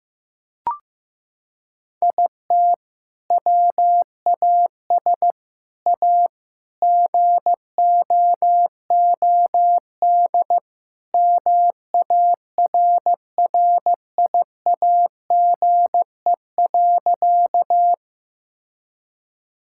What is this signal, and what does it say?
Morse code, heard as a keyed tone, about 15 wpm, 700 Hz. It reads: IT WAS A GOOD MARRIAGE.
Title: Sentences from Top 1000 Words - Review Part 5 - 15wpm